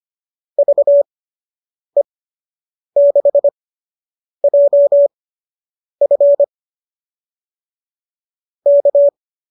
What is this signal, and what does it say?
Morse code, heard as a keyed tone, about 25 wpm, spaced as for 8 wpm, 590 Hz. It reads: VE6JF K